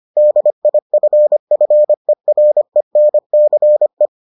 Morse code, heard as a keyed tone, about 25 wpm, 600 Hz. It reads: DIFFERENCE